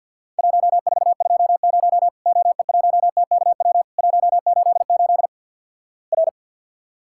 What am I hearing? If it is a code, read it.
Morse code, heard as a keyed tone, about 50 wpm, 700 Hz. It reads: 1420 OE1TKW 188 R
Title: Worked All Europe DX Contest - QTC Exchange with Repeats - 50wpm